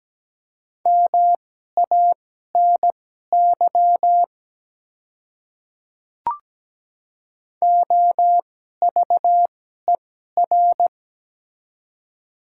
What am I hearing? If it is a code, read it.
Morse code, heard as a keyed tone, about 17 wpm, spaced as for 12 wpm, 700 Hz. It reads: MANY OVER
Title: 2x Character Spacing - Top 100 Words - 17wpm